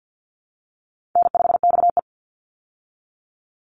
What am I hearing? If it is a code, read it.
Morse code, heard as a keyed tone, about 50 wpm, 700 Hz. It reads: N5XE
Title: Easy Call Signs - 50wpm